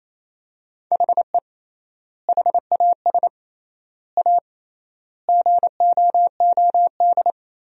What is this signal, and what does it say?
Morse code, heard as a keyed tone, about 28 wpm, 700 Hz. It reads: HE HAS A GOOD
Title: 3x Word-Spacing - Sentences from Top 500 Words - 28wpm